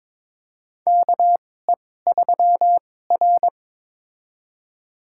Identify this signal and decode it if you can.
Morse code, heard as a keyed tone, about 22 wpm, spaced as for 16 wpm, 700 Hz. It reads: KE3R
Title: USA Call Signs - 4 Characters - 2x Character Spacing - 22wpm